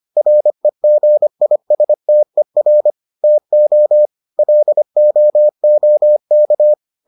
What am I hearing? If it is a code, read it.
Morse code, heard as a keyed tone, about 25 wpm, 600 Hz. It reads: REGISTER TO LOOK